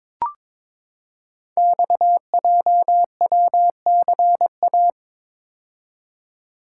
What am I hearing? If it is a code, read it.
Morse code, heard as a keyed tone, about 22 wpm, 700 Hz. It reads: XJWCA
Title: Code Groups - 5 Letter:Numbers - 22wpm